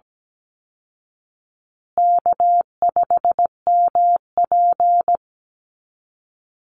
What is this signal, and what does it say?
Morse code, heard as a keyed tone, about 17 wpm, 700 Hz. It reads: K5MP
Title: Easy Call Signs Repeated 3x - 17wpm